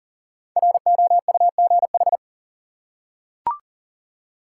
Morse code, heard as a keyed tone, about 38 wpm, 700 Hz. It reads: ROUGH
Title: Five Letter Words - 40wpm